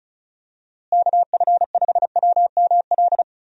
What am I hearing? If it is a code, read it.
Morse code, heard as a keyed tone, about 36 wpm, 700 Hz. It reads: KF5WML
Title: USA Call Signs - 6 Characters - Repeated 3x - 35wpm